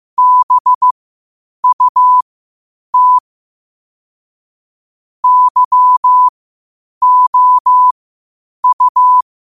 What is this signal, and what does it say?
Morse code, heard as a keyed tone, about 15 wpm, spaced as for 8 wpm, 1000 Hz. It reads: BUT YOU
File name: 8wpm